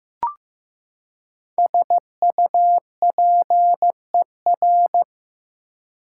Morse code, heard as a keyed tone, about 15 wpm, 700 Hz. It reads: SUPER
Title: Prefixes and Suffixes 15wpm